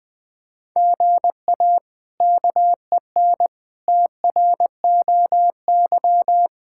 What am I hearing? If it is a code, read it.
Morse code, heard as a keyed tone, about 20 wpm, 700 Hz. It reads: GA KEN TROY
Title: K1USN SST QSO Practice - 20wpm